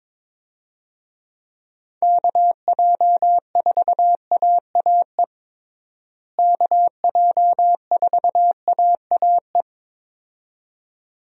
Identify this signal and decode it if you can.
Morse code, heard as a keyed tone, about 22 wpm, 700 Hz. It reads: KJ4AAE KJ4AAE
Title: USA Call Signs - 6 Characters - Repeated 2x - 22wpm